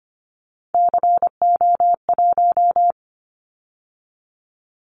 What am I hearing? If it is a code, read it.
Morse code, heard as a keyed tone, about 25 wpm, 700 Hz. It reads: CO1
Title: Callsign Prefixes for Top 100 DXCC - 25wpm